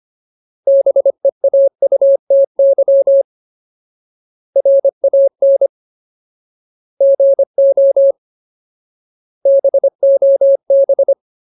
Morse code, heard as a keyed tone, about 25 wpm, 550 Hz. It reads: BEAUTY RAN GO BOB
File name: Ultimate Word List / 25WPM Ultimate